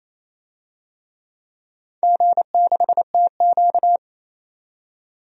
Morse code, heard as a keyed tone, about 28 wpm, 700 Hz. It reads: G6TQ
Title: DXCC-100 Call Signs - 4 Characters - 28wpm